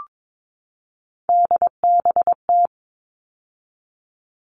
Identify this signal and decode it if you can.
Morse code, heard as a keyed tone, about 22 wpm, 700 Hz. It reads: DBT
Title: All Cut Numbers - 3-Digit 22wpm